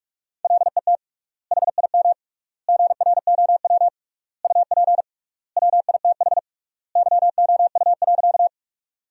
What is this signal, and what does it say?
Morse code, heard as a keyed tone, about 44 wpm, 700 Hz. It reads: LET HIM GROW UP WITH YOU.
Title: 3x Word-Spacing - Sentences from Top 300 Words - 45wpm